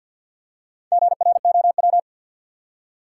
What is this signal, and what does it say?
Morse code, heard as a keyed tone, about 53 wpm, 700 Hz. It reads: GROW
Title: Four Letter Words - 50wpm